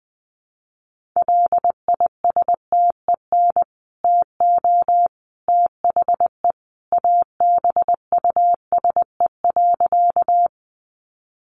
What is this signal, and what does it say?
Morse code, heard as a keyed tone, about 20 wpm, 700 Hz. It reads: LISTEN TO THE ABUSE.
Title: Sentences from Top 1300 Words 20wpm